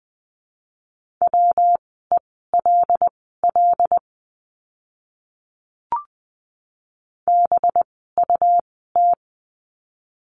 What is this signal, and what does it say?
Morse code, heard as a keyed tone, about 20 wpm, spaced as for 14 wpm, 700 Hz. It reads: WELL BUT
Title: 206 Lesson 33f - Letter j - Words 20wpm